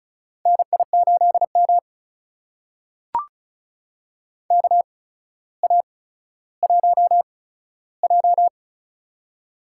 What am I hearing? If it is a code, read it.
Morse code, heard as a keyed tone, about 34 wpm, spaced as for 15 wpm, 700 Hz. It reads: NI8M KA1J
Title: USA Call Signs - 8x Character Spacing - 35wpm